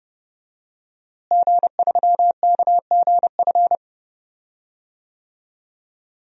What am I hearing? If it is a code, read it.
Morse code, heard as a keyed tone, about 30 wpm, 700 Hz. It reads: G3KGF
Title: DXCC-100 Call Signs - 5 Characters - Repeated 2x - 30wpm